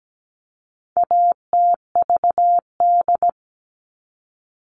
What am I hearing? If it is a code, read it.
Morse code, heard as a keyed tone, about 17 wpm, 700 Hz. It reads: ATVD
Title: All Cut Numbers - 4-Digit 17wpm